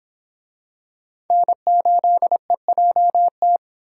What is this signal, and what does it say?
Morse code, heard as a keyed tone, about 25 wpm, 700 Hz. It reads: N8EJT